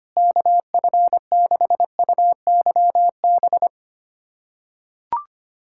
Morse code, heard as a keyed tone, about 25 wpm, 700 Hz. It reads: KF6UYB
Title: USA Callsign DE Callsign - 6 Characters - 25wpm